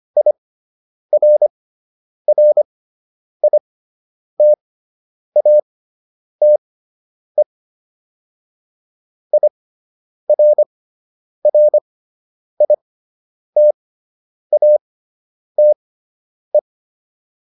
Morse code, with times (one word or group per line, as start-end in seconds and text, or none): 0.2-7.4 IRRITATE
9.3-16.6 IRRITATE